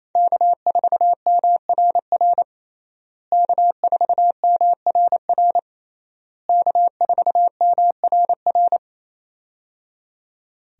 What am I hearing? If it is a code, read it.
Morse code, heard as a keyed tone, about 28 wpm, 700 Hz. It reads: K4MRR K4MRR K4MRR
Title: USA Call Signs - 5 Characters - Repeated 3x - 28wpm